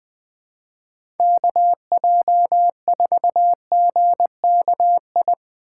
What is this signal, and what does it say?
Morse code, heard as a keyed tone, about 20 wpm, 700 Hz. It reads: KJ4GKI
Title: USA Call Signs - 6 Characters - 3x Character Spacing - 20wpm